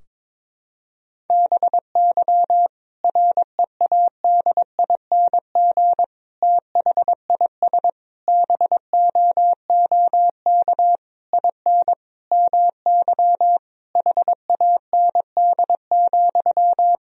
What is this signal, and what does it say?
Morse code, heard as a keyed tone, about 22 wpm, 700 Hz. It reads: BY READING THIS BOOK IN MY HAND,
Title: Sentences from Top 1000 Words - Review Part 3 - 22wpm